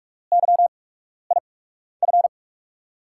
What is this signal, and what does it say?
Morse code, heard as a keyed tone, about 44 wpm, spaced as for 12 wpm, 700 Hz. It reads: YIF
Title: USA Call Signs - 5 Characters - 8x Character Spacing - 45wpm